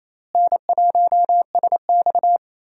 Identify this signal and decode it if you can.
Morse code, heard as a keyed tone, about 28 wpm, 700 Hz. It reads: N1SX
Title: USA Call Signs - 4 Characters - Repeated 3x - 28wpm